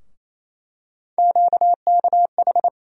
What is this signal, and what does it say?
Morse code, heard as a keyed tone, about 28 wpm, 700 Hz. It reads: QKH
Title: USA Call Sign Postfixes - 3 Letters 28wpm